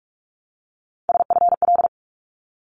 Morse code, heard as a keyed tone, about 44 wpm, 700 Hz. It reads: SFL
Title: ARRL Sections 45wpm